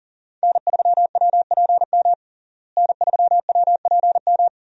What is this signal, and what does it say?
Morse code, heard as a keyed tone, about 40 wpm, 700 Hz. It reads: N3WPM N3WPM
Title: USA Call Signs - 5 Characters - Repeated 3x - 40wpm